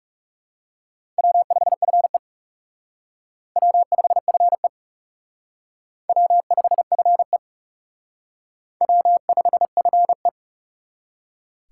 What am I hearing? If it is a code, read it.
Morse code, heard as a keyed tone, about 39 wpm, 700 Hz. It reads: W5FE W5FE W5FE W5FE